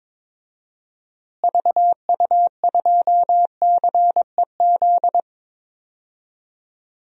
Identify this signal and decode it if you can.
Morse code, heard as a keyed tone, about 22 wpm, 700 Hz. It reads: VU2CEZ
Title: DXCC-100 Call Signs - 6 Characters - Repeated 2x - 22wpm